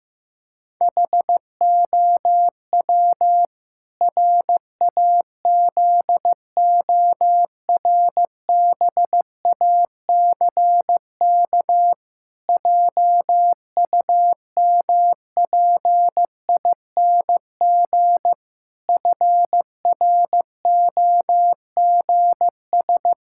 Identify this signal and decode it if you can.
Morse code, heard as a keyed tone, about 16 wpm, 700 Hz. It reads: HOW RAZORBACK JUMPING FROGS
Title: Pangram Challenge v2 - 15wpm